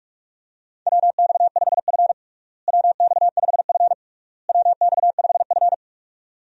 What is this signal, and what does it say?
Morse code, heard as a keyed tone, about 42 wpm, 700 Hz. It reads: WX5F WX5F WX5F
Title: USA Call Signs - 4 Characters - Repeated 3x - 45wpm